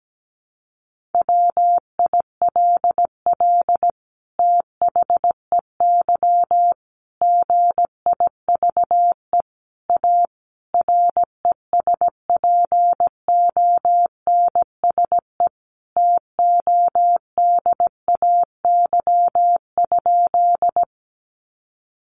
Morse code, heard as a keyed tone, about 17 wpm, 700 Hz. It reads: WILL THEY GIVE A RESPONSE TODAY?
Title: Sentences from Top 1400 Words 17wpm